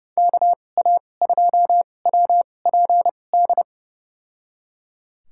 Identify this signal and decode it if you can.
Morse code, heard as a keyed tone, about 32 wpm, spaced as for 22 wpm, 700 Hz. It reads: KA2WPD
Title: USA Call Signs - 2x Character Spacing - 30wpm